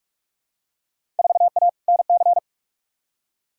Morse code, heard as a keyed tone, about 43 wpm, 700 Hz. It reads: 4A NC
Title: Field Day Exchanges - 45wpm